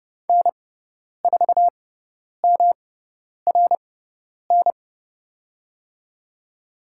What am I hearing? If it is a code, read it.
Morse code, heard as a keyed tone, about 30 wpm, spaced as for 10 wpm, 700 Hz. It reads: N4MRN
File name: Lesson 18g - Letter m - Call Signs 30wpm